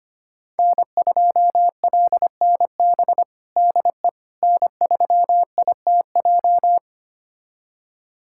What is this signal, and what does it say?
Morse code, heard as a keyed tone, about 25 wpm, 700 Hz. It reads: N2LNB DE N3ITJ
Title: USA Callsign DE Callsign - 5 Characters - 25wpm